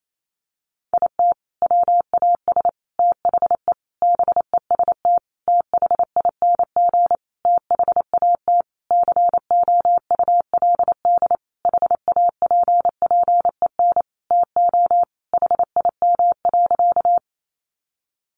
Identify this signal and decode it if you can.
Morse code, heard as a keyed tone, about 28 wpm, 700 Hz. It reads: IT WAS THE BEST THING THAT COULD HAPPEN TO HIM.
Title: Sentences from Top 400 Words 28wpm